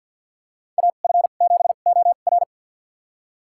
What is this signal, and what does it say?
Morse code, heard as a keyed tone, about 52 wpm, spaced as for 36 wpm, 700 Hz. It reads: AF7OR
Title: USA Call Signs - 5 Characters - 2x Character Spacing - 50wpm